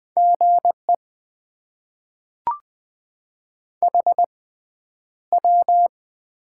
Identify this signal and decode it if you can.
Morse code, heard as a keyed tone, about 20 wpm, spaced as for 10 wpm, 700 Hz. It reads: GE HW
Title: Top 90 QSO Elements - 6x Character Spacing - 20wpm